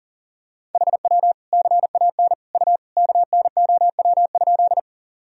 Morse code, heard as a keyed tone, about 40 wpm, 700 Hz. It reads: HW CAN U KNOW?
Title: Sentences from Top 100 Words and Abbreviations 40wpm